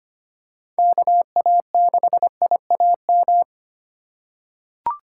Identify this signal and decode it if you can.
Morse code, heard as a keyed tone, about 25 wpm, 700 Hz. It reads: KA6IAM
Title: USA Call Signs - 6 Characters - 6x Character Spacing - 25wpm